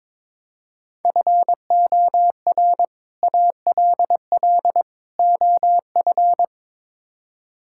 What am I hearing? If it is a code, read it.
Morse code, heard as a keyed tone, about 22 wpm, 700 Hz. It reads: FOR ALL OF
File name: practice / 22wpm / Build A Phrase - 5-Word Phrases - 22wpm